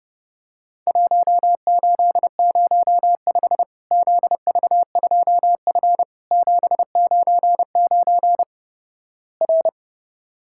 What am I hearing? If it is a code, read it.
Morse code, heard as a keyed tone, about 31 wpm, 700 Hz. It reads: 1805 ZV2F 799 R
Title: Worked All Europe DX Contest - QTC Exchange - 30wpm